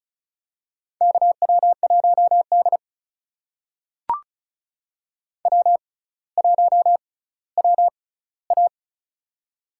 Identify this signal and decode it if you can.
Morse code, heard as a keyed tone, about 36 wpm, spaced as for 18 wpm, 700 Hz. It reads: KW1D W1WA
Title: USA Call Signs - 4 Characters - 6x Character Spacing - 35wpm